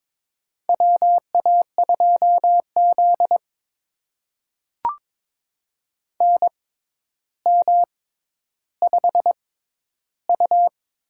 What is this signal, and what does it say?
Morse code, heard as a keyed tone, about 22 wpm, spaced as for 11 wpm, 700 Hz. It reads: WA2Z NM5U
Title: USA Call Signs - 4 Characters - 6x Character Spacing - 22wpm